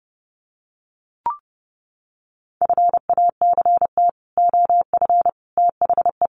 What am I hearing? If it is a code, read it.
Morse code, heard as a keyed tone, about 30 wpm, 700 Hz. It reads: FACT OF THE
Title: Sets of 5 Words 30wpm